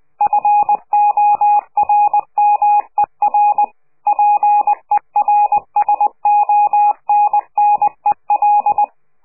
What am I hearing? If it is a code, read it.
Morse code, heard as a keyed tone, about 20 wpm, 830 Hz. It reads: FORMER PERSONNEL
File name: CQ de PCH laatste bericht farewell